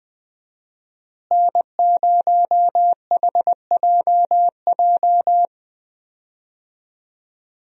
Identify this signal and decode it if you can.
Morse code, heard as a keyed tone, about 20 wpm, 700 Hz. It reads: N0HJJ
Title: USA Call Signs - 5 Characters - Repeated 2x - 20wpm